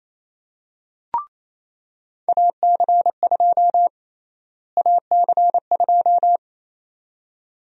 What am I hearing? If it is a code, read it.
Morse code, heard as a keyed tone, about 28 wpm, 700 Hz. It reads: AC2 AC2
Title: USA Call Sign Prefixes - Repeated 2x - 28wpm